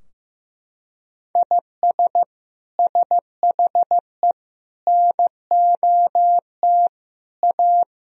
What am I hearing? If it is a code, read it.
Morse code, heard as a keyed tone, about 15 wpm, 700 Hz. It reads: IS SHE NOT A
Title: Sentences from Top 1000 Words - Review Part 2 - 15wpm